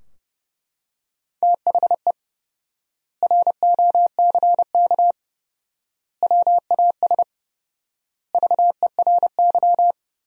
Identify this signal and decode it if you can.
Morse code, heard as a keyed tone, about 30 wpm, 700 Hz. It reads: THE ROCK WAS VERY
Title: 4x Word-Spacing - Sentences from Top 400 Words - 30wpm